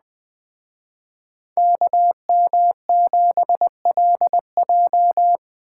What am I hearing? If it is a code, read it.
Morse code, heard as a keyed tone, about 21 wpm, 700 Hz. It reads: KM7LJ